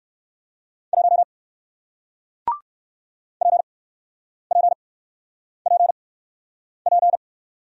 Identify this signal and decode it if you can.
Morse code, heard as a keyed tone, about 61 wpm, spaced as for 9 wpm, 700 Hz. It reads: 1 PPPP